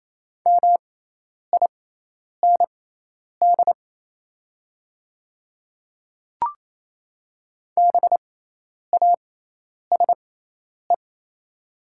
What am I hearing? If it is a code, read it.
Morse code, heard as a keyed tone, about 28 wpm, spaced as for 10 wpm, 700 Hz. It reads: MIND BASE
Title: 6x Character Spacing - Top 500 Words - 28wpm